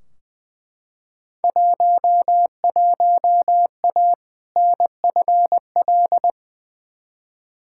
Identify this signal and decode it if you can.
Morse code, heard as a keyed tone, about 20 wpm, 700 Hz. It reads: 11A NFL
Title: Field Day Exchanges Encore - 20wpm